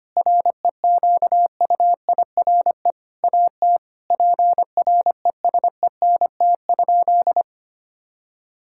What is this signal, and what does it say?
Morse code, heard as a keyed tone, about 25 wpm, 700 Hz. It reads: REQUIRE AT PRESENT?
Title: Sentences from Top 1000 Words - Review Part 6 - 25wpm